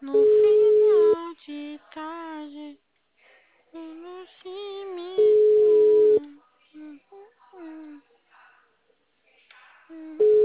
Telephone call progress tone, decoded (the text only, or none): ringback tone